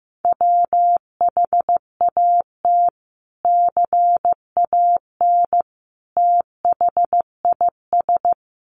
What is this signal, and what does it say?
Morse code, heard as a keyed tone, about 15 wpm, 700 Hz. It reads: WHAT CAN THIS
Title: Sentences from Top 100 Words 15wpm